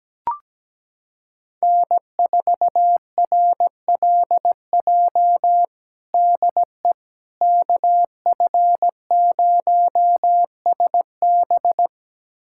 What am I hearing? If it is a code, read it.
Morse code, heard as a keyed tone, about 17 wpm, 700 Hz. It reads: N4RLJ DE KF0SB
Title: USA Callsign DE Callsign - 5 Characters - 17wpm